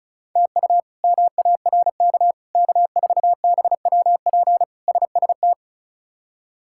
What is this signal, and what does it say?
Morse code, heard as a keyed tone, about 36 wpm, 700 Hz. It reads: TU MARK K4BWP SST
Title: K1USN SST QSO Practice - 35wpm